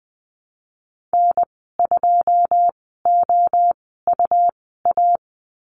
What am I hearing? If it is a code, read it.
Morse code, heard as a keyed tone, about 20 wpm, spaced as for 15 wpm, 700 Hz. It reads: N2OUA
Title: 193 Lesson 31g - Letter v - Call Signs 20wpm